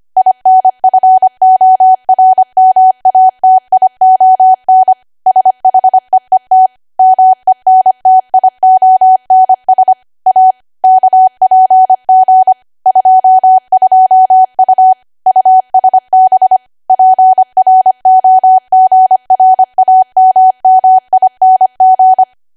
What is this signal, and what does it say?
Morse code, heard as a keyed tone, about 25 wpm, 750 Hz. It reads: INFORMATION SHEET MENTIONS A KPG 22U USB PROGRAMMING
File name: Archive / 25 WPM / 25WPM